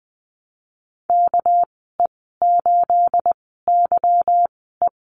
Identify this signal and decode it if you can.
Morse code, heard as a keyed tone, about 20 wpm, spaced as for 15 wpm, 700 Hz. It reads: KE8YE
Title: 213 Lesson 34g - Number 8 - Call Signs 20wpm